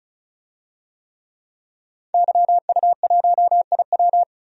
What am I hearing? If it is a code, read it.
Morse code, heard as a keyed tone, about 34 wpm, 700 Hz. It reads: YU1IW
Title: DXCC-100 Call Signs - 5 Characters - 35wpm